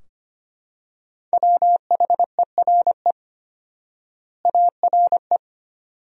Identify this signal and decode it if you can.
Morse code, heard as a keyed tone, about 25 wpm, 700 Hz. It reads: WHERE ARE
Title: 4x Word-Spacing - Sentences from Top 400 Words - 25wpm